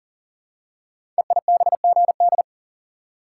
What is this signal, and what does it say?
Morse code, heard as a keyed tone, about 38 wpm, 700 Hz. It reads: EIBGD
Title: Code Groups - 5 Letters - 40wpm